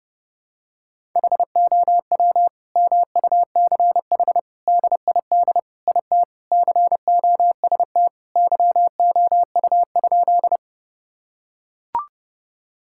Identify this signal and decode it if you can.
Morse code, heard as a keyed tone, about 30 wpm, 700 Hz. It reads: HOW MUCH DID IT COST YOU?
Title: Sentences from Top 1000 Words - Review Part 5 - 30wpm